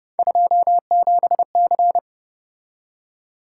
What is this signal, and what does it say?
Morse code, heard as a keyed tone, about 30 wpm, 700 Hz. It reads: 27C
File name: Speed-Racing - Three Letter-Number Combinations - 25wpm